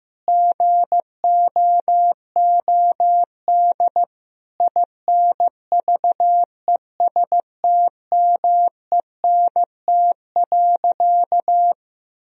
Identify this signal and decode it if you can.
Morse code, heard as a keyed tone, about 15 wpm, 700 Hz. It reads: GOOD INVESTMENT.
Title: Sentences from Top 1000 Words - Review Part 4 - 15wpm